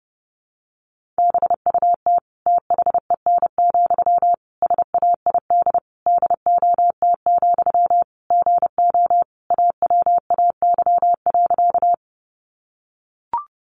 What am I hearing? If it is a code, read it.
Morse code, heard as a keyed tone, about 30 wpm, 700 Hz. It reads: BUT THEN, SAID DOT, GO AWAY.
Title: Sentences from Top 1500 Words 30wpm